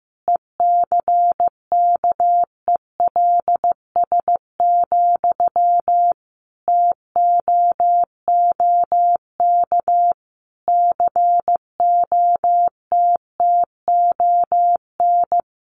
Morse code, heard as a keyed tone, about 14 wpm, 700 Hz. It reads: ECKELS, TOOK COTTON
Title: A Sound Of Thunder 15wpm